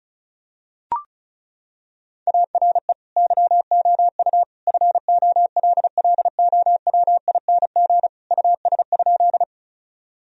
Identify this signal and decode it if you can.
Morse code, heard as a keyed tone, about 36 wpm, 700 Hz. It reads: ARE YOU FOLLOWING US?
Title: Sentences from Top 1000 Words - Review Part 4 - 35wpm